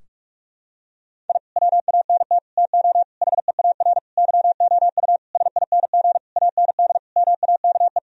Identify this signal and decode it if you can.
Morse code, heard as a keyed tone, about 41 wpm, 700 Hz. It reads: I WANT TO HEAR YOU SING AND MAKE
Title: Sentences from Top 1000 Words - Review Part 4 - 45wpm